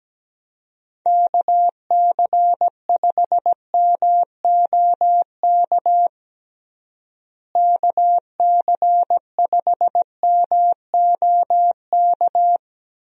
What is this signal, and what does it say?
Morse code, heard as a keyed tone, about 17 wpm, 700 Hz. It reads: KC5MOK KC5MOK
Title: USA Call Signs - 6 Characters - Repeated 3x - 17wpm